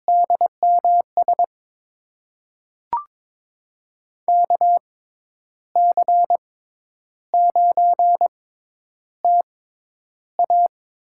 Morse code, heard as a keyed tone, about 21 wpm, spaced as for 10 wpm, 700 Hz. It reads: DMS KC9TA
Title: USA Call Signs - 6 Characters - 6x Character Spacing - 22wpm